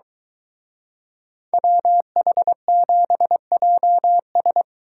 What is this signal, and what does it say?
Morse code, heard as a keyed tone, about 23 wpm, 700 Hz. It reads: WH7JS